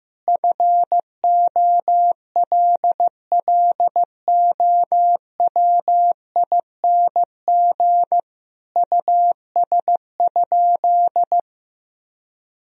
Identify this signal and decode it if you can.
Morse code, heard as a keyed tone, about 15 wpm, 700 Hz. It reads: FOLLOWING US?
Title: Sentences from Top 1000 Words - Review Part 4 - 15wpm